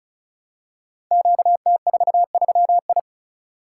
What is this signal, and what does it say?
Morse code, heard as a keyed tone, about 36 wpm, 700 Hz. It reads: QT43I
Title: Code Groups - 5 Letter:Numbers - 35wpm